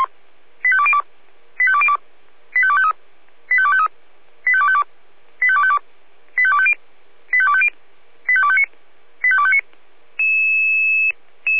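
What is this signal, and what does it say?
Five-tone selective call (ZVEI), heard as quick runs of five tones, tones of about 70 ms. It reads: 85202 85202 85253 85253 85252 85252 85250 85250 85259 85259
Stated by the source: ZVEI five-tone call 85102 received on an SDR.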